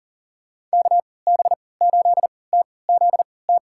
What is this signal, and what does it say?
Morse code, heard as a keyed tone, about 40 wpm, spaced as for 23 wpm, 700 Hz. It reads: KB8TZT
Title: USA Call Signs - 3x Character Spacing - 40wpm